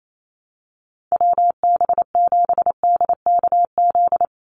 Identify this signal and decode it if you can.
Morse code, heard as a keyed tone, about 28 wpm, 700 Hz. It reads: WB7DKZ
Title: Call Signs - 28wpm